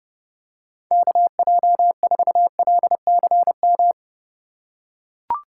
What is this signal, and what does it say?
Morse code, heard as a keyed tone, about 30 wpm, 700 Hz. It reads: KJ4LCM
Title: USA Call Signs - 6 Characters - 2x Character Spacing - 30wpm